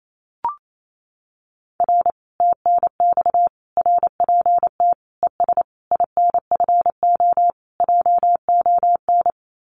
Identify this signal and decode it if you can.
Morse code, heard as a keyed tone, about 28 wpm, 700 Hz. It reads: R TNX RPT ES INFO JON
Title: Common QSO Phrases Encore 28wpm